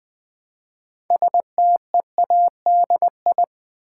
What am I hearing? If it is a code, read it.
Morse code, heard as a keyed tone, about 21 wpm, 700 Hz. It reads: STEADI
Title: Build A Word - 2 Letters At A Time - 8-Letter Words - 20wpm